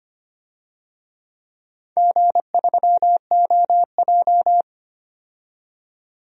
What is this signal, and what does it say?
Morse code, heard as a keyed tone, about 25 wpm, 700 Hz. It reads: G3OJ
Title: DXCC-100 Call Signs - 4 Characters - 25wpm